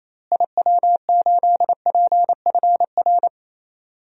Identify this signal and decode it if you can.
Morse code, heard as a keyed tone, about 28 wpm, 700 Hz. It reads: IW8PFR
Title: DXCC-100 Call Signs - 6 Characters - Repeated 3x - 28wpm